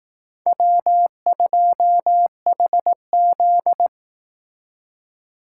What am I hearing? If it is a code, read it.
Morse code, heard as a keyed tone, about 19 wpm, 700 Hz. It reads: W2HZ
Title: Speed Racing - USA 1x2 Call Signs 15wpm